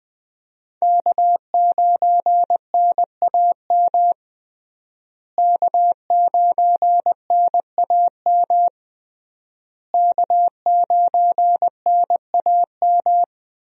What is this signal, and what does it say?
Morse code, heard as a keyed tone, about 20 wpm, 700 Hz. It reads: K9NAM K9NAM K9NAM
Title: USA Call Signs - 5 Characters - Repeated 3x - 20wpm